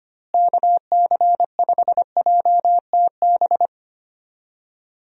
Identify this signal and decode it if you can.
Morse code, heard as a keyed tone, about 25 wpm, 700 Hz. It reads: KC5JTB
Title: USA Call Signs - 6 Characters - Repeated 3x - 25wpm